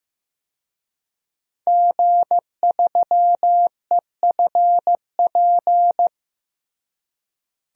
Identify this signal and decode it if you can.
Morse code, heard as a keyed tone, about 15 wpm, 700 Hz. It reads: G3EFP
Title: DXCC-100 Call Signs - 5 Characters - Repeated 3x - 15wpm